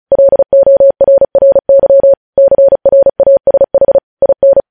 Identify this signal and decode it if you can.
Morse code, heard as a keyed tone, about 36 wpm, 550 Hz. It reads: LORRY CRASH IN
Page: News Headlines in Morse Code at 35 WPM